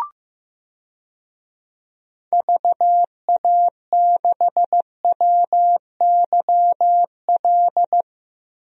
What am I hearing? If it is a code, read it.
Morse code, heard as a keyed tone, about 15 wpm, 700 Hz. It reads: VA6WYL
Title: DXCC-100 Call Signs - 6 Characters - 15wpm